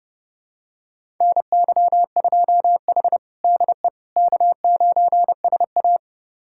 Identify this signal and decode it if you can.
Morse code, heard as a keyed tone, about 30 wpm, 700 Hz. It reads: NY2H DE K9SA
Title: USA Callsign DE Callsign - 4 Characters - 30wpm